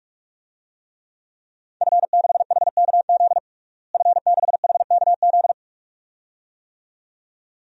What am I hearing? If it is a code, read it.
Morse code, heard as a keyed tone, about 46 wpm, 700 Hz. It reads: F6HKZ F6HKZ
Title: DXCC-100 Call Signs - 5 Characters - Repeated 2x - 45wpm